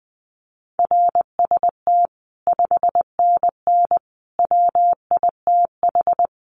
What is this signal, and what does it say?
Morse code, heard as a keyed tone, about 20 wpm, 700 Hz. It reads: RST 5NN WITH
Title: Common QSO Phrases 20wpm